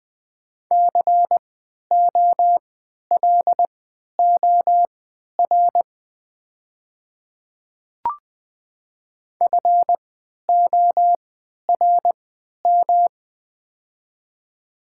Morse code, heard as a keyed tone, about 20 wpm, spaced as for 11 wpm, 700 Hz. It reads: COLOR FORM
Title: 3x Character Spacing - Top 300 Words - 20wpm